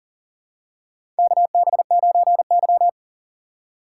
Morse code, heard as a keyed tone, about 40 wpm, 700 Hz. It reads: KB9Y
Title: USA Call Signs - 4 Characters - 2x Character Spacing - 40wpm